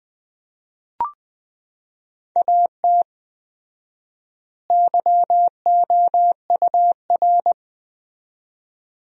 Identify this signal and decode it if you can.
Morse code, heard as a keyed tone, about 21 wpm, 700 Hz. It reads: AT YOUR
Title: 4x Word-Spacing - Sentences from Top 200 Words - 20wpm